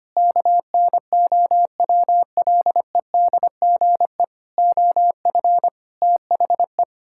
Morse code, heard as a keyed tone, about 25 wpm, 700 Hz. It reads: KNOWLEDGE OF THE
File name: practice / 25wpm / Sentences with 9-Word Phrases - 25wpm